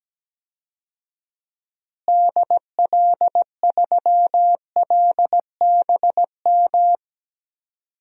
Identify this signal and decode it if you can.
Morse code, heard as a keyed tone, about 17 wpm, 700 Hz. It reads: DL3LBM
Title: DXCC-100 Call Signs - 6 Characters - Repeated 3x - 17wpm